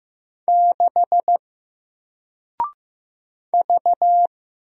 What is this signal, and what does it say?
Morse code, heard as a keyed tone, about 15 wpm, 700 Hz. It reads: 6 V
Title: Single Character (Letter:Number:Punctuation:Prosign) Alt 15wpm